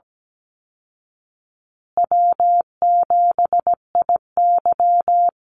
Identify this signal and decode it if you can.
Morse code, heard as a keyed tone, about 17 wpm, 700 Hz. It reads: W7IY
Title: Easy Call Signs Repeated 3x - 17wpm